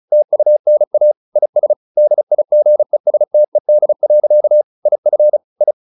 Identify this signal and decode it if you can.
Morse code, heard as a keyed tone, about 34 wpm, 600 Hz. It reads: TUNA IS DIGESTED. IF I
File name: the old man and the sea / the old man and the sea 1 35wpm